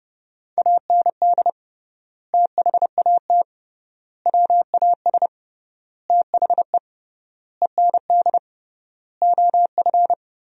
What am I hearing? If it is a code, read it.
Morse code, heard as a keyed tone, about 30 wpm, 700 Hz. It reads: AND THAT WAS THE END OF
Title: 3x Word-Spacing - Sentences from Top 200 Words - 30wpm